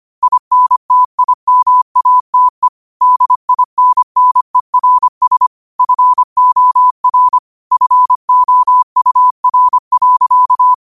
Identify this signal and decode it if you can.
Morse code, heard as a keyed tone, about 25 wpm, 1000 Hz. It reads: INTIMATE DINNERS FOR FOUR.
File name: qotdfiles / 25wpm